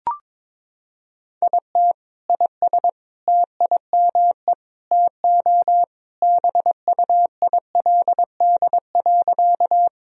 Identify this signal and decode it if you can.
Morse code, heard as a keyed tone, about 22 wpm, 700 Hz. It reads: IT IS TIME TO BUILD.
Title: Sentences from Top 1000 Words - Review Part 1 - 22wpm